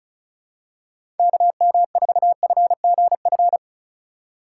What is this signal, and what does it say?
Morse code, heard as a keyed tone, about 36 wpm, 700 Hz. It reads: KM4FGF
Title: USA Call Signs - 6 Characters - 3x Character Spacing - 35wpm